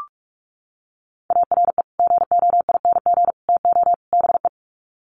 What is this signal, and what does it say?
Morse code, heard as a keyed tone, about 44 wpm, 700 Hz. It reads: ARE GOING TO BE
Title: Sets of 4 Words 45wpm